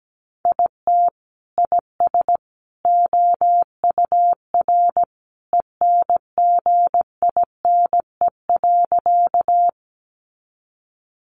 Morse code, heard as a keyed tone, about 17 wpm, 700 Hz. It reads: IT IS OUR ENGINE.
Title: Sentences from Top 1500 Words 17wpm